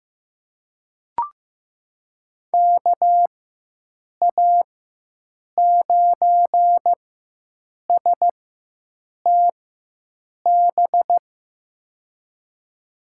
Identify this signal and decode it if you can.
Morse code, heard as a keyed tone, about 15 wpm, spaced as for 7 wpm, 700 Hz. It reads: KA9STB
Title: USA Call Signs - 6 Characters - 4x Character Spacing - 15wpm